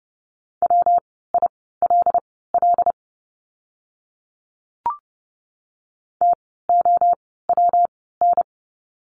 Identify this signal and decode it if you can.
Morse code, heard as a keyed tone, about 30 wpm, spaced as for 17 wpm, 700 Hz. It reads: WILL TOWN
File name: course / 30:10wpm / 115 Lesson 19f - Letter w - Words 30wpm